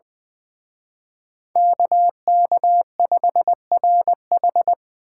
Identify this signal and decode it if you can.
Morse code, heard as a keyed tone, about 21 wpm, 700 Hz. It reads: KK5RH
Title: Speed Racing - USA 2x2 Call Signs 20wpm